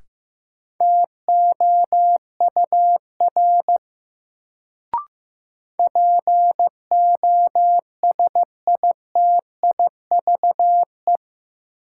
Morse code, heard as a keyed tone, about 15 wpm, 700 Hz. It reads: TOUR POSITIVE
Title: Sentences from Top 1400 Words - New Word Drill Alt - 15wpm